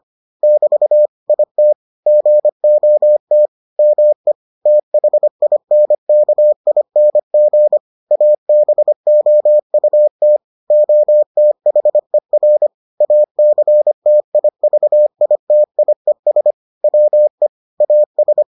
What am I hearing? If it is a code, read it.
Morse code, heard as a keyed tone, about 25 wpm, 600 Hz. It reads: <BT> IT GOT ME THINKING ABOUT OTHER ACTIVITIES WE AS